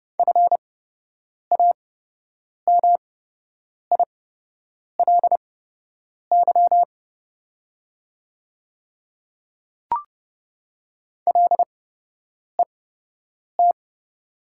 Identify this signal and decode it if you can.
Morse code, heard as a keyed tone, about 29 wpm, spaced as for 8 wpm, 700 Hz. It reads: FAMILY LET
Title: 8x Character Spacing - Top 500 Words - 30wpm